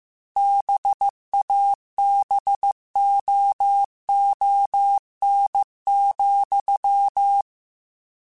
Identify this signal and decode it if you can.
Morse code, heard as a keyed tone, about 15 wpm, 800 Hz. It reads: BABOON,